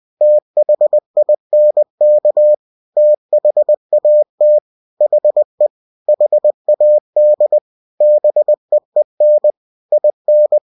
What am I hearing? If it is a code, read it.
Morse code, heard as a keyed tone, about 20 wpm, 600 Hz. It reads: THINK THAT HE HAD BEEN IN
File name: cw-books / winnie-the-pooh / winnie-20wpm-page-0100